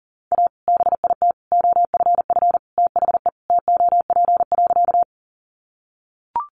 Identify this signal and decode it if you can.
Morse code, heard as a keyed tone, about 40 wpm, 700 Hz. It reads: A BIT OFF THE TOP.
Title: Sentences from Top 900 Words 40wpm